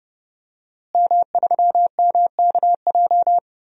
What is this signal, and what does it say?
Morse code, heard as a keyed tone, about 32 wpm, 700 Hz. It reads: M3MKJ
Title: DXCC-100 Call Signs - 5 Characters - Repeated 3x - 30wpm